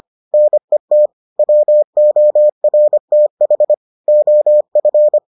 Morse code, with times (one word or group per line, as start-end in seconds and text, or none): 0.3-1.1 NET
1.4-3.7 WORTH
4.1-5.2 OF